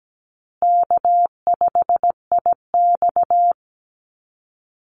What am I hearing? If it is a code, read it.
Morse code, heard as a keyed tone, about 17 wpm, 700 Hz. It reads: K5IX
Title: Call Signs Repeated 2x - 17wpm